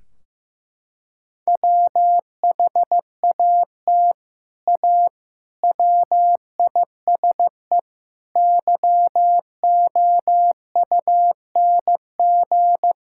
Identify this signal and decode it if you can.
Morse code, heard as a keyed tone, about 16 wpm, 700 Hz. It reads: WHAT A WISE YOUNG
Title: Sentences from Top 1600 Words 15wpm